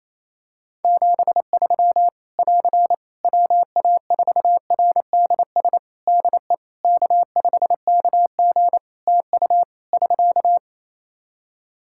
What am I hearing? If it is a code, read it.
Morse code, heard as a keyed tone, about 28 wpm, 700 Hz. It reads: 73 <AR> WA4RDS DE K5KG TU <SK>
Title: Practice QSO - Exchange 4 - Ending - 28wpm